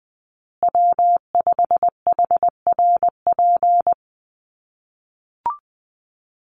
Morse code, heard as a keyed tone, about 20 wpm, 700 Hz. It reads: W5HRP
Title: USA Call Signs Repeated 2x 20wpm